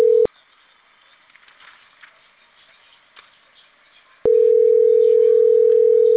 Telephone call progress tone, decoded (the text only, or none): ringback tone